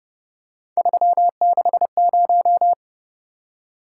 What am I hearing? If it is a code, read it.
Morse code, heard as a keyed tone, about 29 wpm, 700 Hz. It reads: 360